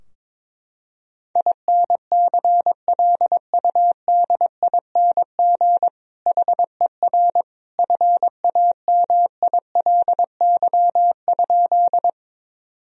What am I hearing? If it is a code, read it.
Morse code, heard as a keyed tone, about 22 wpm, 700 Hz. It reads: INCLUDING HER FAMILY?
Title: Sentences from Top 1000 Words - Review Part 4 - 22wpm